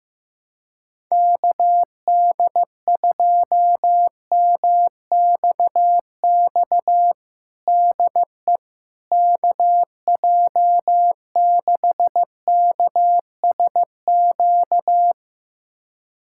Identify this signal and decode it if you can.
Morse code, heard as a keyed tone, about 15 wpm, 700 Hz. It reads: KD2MXX DE KJ6KSQ
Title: USA Callsign DE Callsign 15wpm